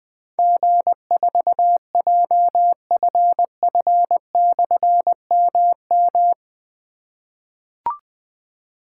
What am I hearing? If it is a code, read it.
Morse code, heard as a keyed tone, about 20 wpm, 700 Hz. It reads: G4JFF/MM
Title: Difficult Call Signs 20wpm